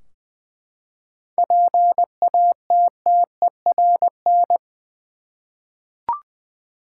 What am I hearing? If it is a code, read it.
Morse code, heard as a keyed tone, about 20 wpm, 700 Hz. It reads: PATTERN
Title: Seven Letter Words - 20wpm